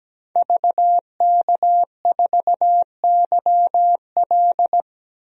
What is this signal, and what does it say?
Morse code, heard as a keyed tone, about 17 wpm, 700 Hz. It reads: VK4YL
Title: DXCC-100 Call Signs - 5 Characters - 17wpm